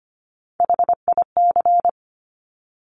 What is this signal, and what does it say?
Morse code, heard as a keyed tone, about 25 wpm, 700 Hz. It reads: HIC
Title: Top Three Letter Combinations in Words 25wpm